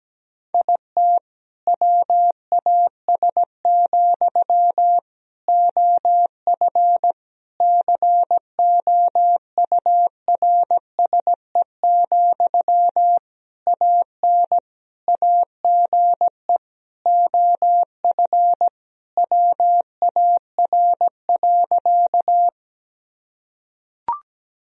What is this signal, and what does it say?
Morse code, heard as a keyed tone, about 17 wpm, 700 Hz. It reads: IT WAS, OF COURSE, AN AGE OF WAR.
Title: Sentences from Top 1000 Words - Review Part 3 - 17wpm